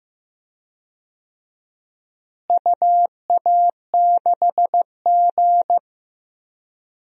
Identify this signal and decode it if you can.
Morse code, heard as a keyed tone, about 15 wpm, 700 Hz. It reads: UA6G
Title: DXCC-100 Call Signs - 4 Characters - Repeated 3x - 15wpm